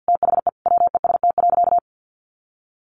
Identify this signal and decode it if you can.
Morse code, heard as a keyed tone, about 50 wpm, 700 Hz. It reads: THE WEST.
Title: Sentences from Top 1200 Words 50wpm